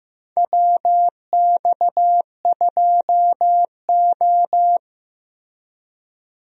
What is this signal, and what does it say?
Morse code, heard as a keyed tone, about 15 wpm, 700 Hz. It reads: WX2O
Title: USA Call Signs - 4 Characters - Repeated 2x - 15wpm